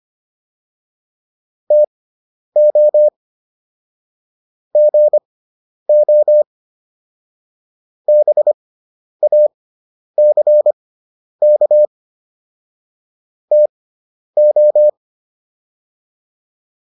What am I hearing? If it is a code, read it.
Morse code, heard as a keyed tone, about 25 wpm, spaced as for 10 wpm, 600 Hz. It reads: TO GO BACK TO